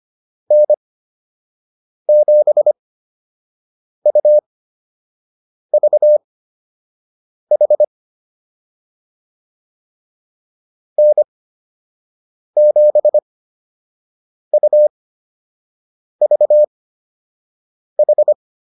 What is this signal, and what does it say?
Morse code, heard as a keyed tone, about 25 wpm, spaced as for 6 wpm, 600 Hz. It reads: N7UVH N7UVH
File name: QSO 105 06